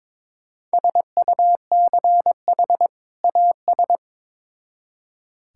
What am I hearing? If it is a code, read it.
Morse code, heard as a keyed tone, about 21 wpm, 700 Hz. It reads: SUCH AS
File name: Build A Phrase - 2-Word Phrases - 22wpm